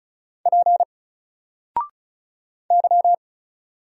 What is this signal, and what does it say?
Morse code, heard as a keyed tone, about 36 wpm, 700 Hz. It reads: P Y
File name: Single-Letter-Number-35wpm